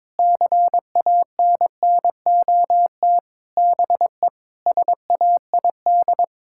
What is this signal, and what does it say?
Morse code, heard as a keyed tone, about 22 wpm, 700 Hz. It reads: CANNOT BE SAID
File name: Sentences with 7-Word Phrases - 22wpm